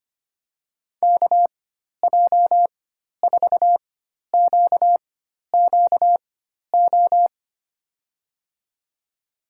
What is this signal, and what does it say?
Morse code, heard as a keyed tone, about 25 wpm, spaced as for 12 wpm, 700 Hz. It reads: KJ4QQO
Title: USA Call Signs - 6 Characters - 4x Character Spacing - 25wpm